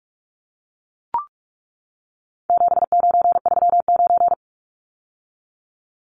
Morse code, heard as a keyed tone, about 44 wpm, 700 Hz. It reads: 7939
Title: Numbers - 4-Digit 45wpm